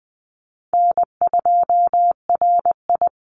Morse code, heard as a keyed tone, about 20 wpm, 700 Hz. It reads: N2RI
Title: Call Signs Repeated 3x - 20wpm